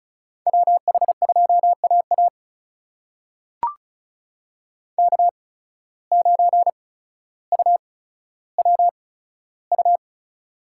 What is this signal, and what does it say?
Morse code, heard as a keyed tone, about 36 wpm, spaced as for 15 wpm, 700 Hz. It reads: WH2AA K9UWU
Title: USA Call Signs - 5 Characters - 8x Character Spacing - 35wpm